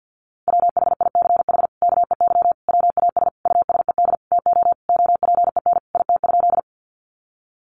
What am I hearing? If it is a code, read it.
Morse code, heard as a keyed tone, about 50 wpm, 700 Hz. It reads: WHICH KEY WAS USED TO OPEN IT?
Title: Sentences from Top 600 Words 50wpm